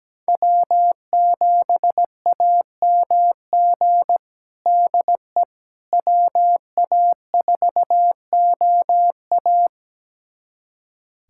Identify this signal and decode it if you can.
Morse code, heard as a keyed tone, about 17 wpm, 700 Hz. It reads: W7AMG DE WA4OA
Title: USA Callsign DE Callsign - 5 Characters - 17wpm